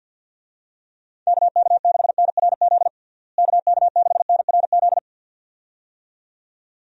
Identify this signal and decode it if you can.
Morse code, heard as a keyed tone, about 53 wpm, 700 Hz. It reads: KK6NRZ KK6NRZ
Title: USA Call Signs - 6 Characters - Repeated 2x - 50wpm